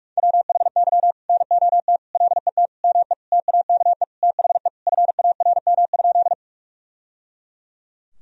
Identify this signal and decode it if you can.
Morse code, heard as a keyed tone, about 41 wpm, 700 Hz. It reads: WHY NOT LET ME TAKE THE FARM?
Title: Sentences from Top 1000 Words - Review Part 2 - 45wpm